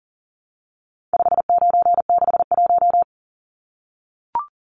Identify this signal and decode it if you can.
Morse code, heard as a keyed tone, about 40 wpm, 700 Hz. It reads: 5961
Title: Numbers - 4-Digit 40wpm